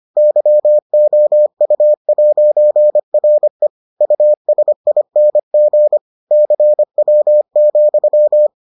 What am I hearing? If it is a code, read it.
Morse code, heard as a keyed tone, about 25 wpm, 600 Hz. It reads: YOU'RE USING CW,